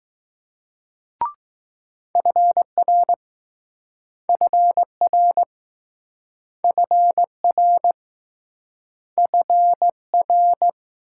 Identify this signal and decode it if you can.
Morse code, heard as a keyed tone, about 20 wpm, 700 Hz. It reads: FR FR FR FR